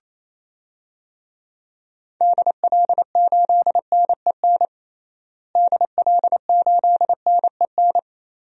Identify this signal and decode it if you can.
Morse code, heard as a keyed tone, about 28 wpm, 700 Hz. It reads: DL8NEN DL8NEN
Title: DXCC-100 Call Signs - 6 Characters - Repeated 3x - 28wpm